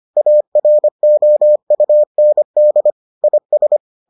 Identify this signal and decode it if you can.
Morse code, heard as a keyed tone, about 25 wpm, 600 Hz. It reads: AROUND IS